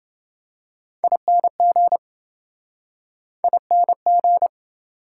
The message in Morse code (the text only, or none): ING ING